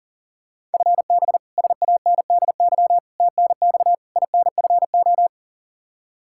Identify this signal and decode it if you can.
Morse code, heard as a keyed tone, about 40 wpm, 690 Hz. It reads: FB SANDY TNX INFO
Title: Practice QSO - Exchange 3 - More Info - 40wpm